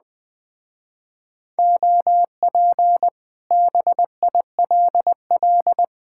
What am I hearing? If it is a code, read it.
Morse code, heard as a keyed tone, about 20 wpm, 700 Hz. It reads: OP BILL
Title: Speed-Racing Common QSO Phrases Encore - 17wpm